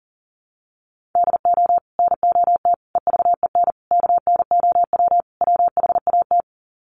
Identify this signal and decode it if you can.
Morse code, heard as a keyed tone, about 40 wpm, 700 Hz. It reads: DO NOT EVEN KNOW WHAT
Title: Sets of 5 Words Encore 40wpm